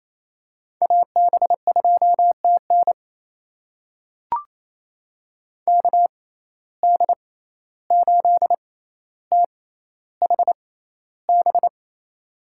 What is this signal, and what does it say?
Morse code, heard as a keyed tone, about 28 wpm, spaced as for 14 wpm, 700 Hz. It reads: AB2TN KD8THB
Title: USA Call Signs - 6x Character Spacing - 28wpm